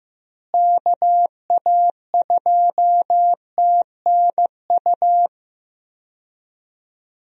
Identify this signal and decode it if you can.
Morse code, heard as a keyed tone, about 15 wpm, 700 Hz. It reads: KA2TNU
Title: USA Call Signs - 6 Characters - Repeated 2x - 15wpm